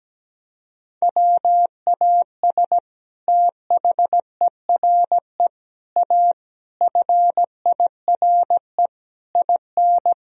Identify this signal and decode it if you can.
Morse code, heard as a keyed tone, about 17 wpm, 700 Hz. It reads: WAS THERE A FIRE IN
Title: Sentences from Top 1000 Words - Review Part 4 - 17wpm